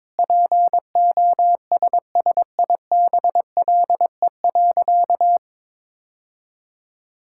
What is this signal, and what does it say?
Morse code, heard as a keyed tone, about 22 wpm, 700 Hz. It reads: POSSIBLE.
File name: practice / 22wpm / Sentences with 3-Word Phrases - 22wpm